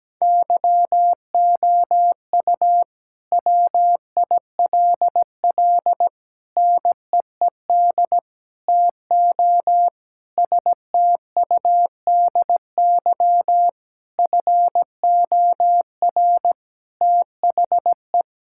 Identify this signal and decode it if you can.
Morse code, heard as a keyed tone, about 17 wpm, 700 Hz. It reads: YOU WILL NEED TO STUDY FOR THE
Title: Sentences from Top 1000 Words - Review Part 5 - 17wpm